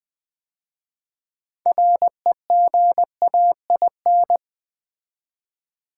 Morse code, heard as a keyed tone, about 20 wpm, 700 Hz. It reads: REGAIN
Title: 6 Letter Words - First 3 Letters Given - 20wpm